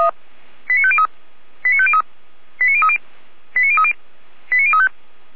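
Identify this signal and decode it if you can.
Five-tone selective call (ZVEI), heard as quick runs of five tones, tones of about 70 ms. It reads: 89503 89503 89030 89030 89035
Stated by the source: Five-tone ZVEI selective call received on an SDR.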